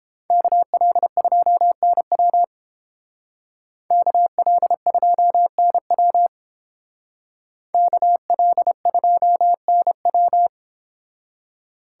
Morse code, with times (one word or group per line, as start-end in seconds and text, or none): 0.3-2.4 KL2NW
3.9-6.3 KL2NW
7.7-10.5 KL2NW